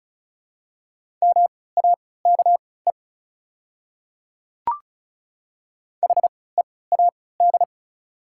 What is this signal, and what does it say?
Morse code, heard as a keyed tone, about 36 wpm, spaced as for 20 wpm, 690 Hz. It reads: MAKE HEAD
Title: 3x Character Spacing - Top 500 Words - 35wpm